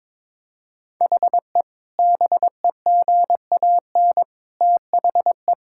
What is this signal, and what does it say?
Morse code, heard as a keyed tone, about 22 wpm, 700 Hz. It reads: HE BEGAN THE
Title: Sentences from Top 1000 Words - Review Part 5 - 22wpm